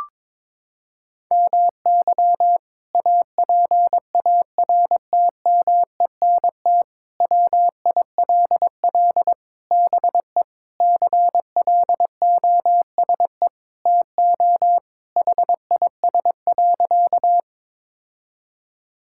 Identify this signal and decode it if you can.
Morse code, heard as a keyed tone, about 23 wpm, 700 Hz. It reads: MY APARTMENT WILL BE CLOSE TO HIS.
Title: Sentences from Top 1000 Words - Review Part 6 - 22wpm